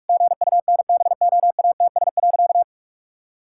Morse code, heard as a keyed tone, about 46 wpm, 700 Hz. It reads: GUNBOATS.
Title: Pangram Challenge v2 - 45wpm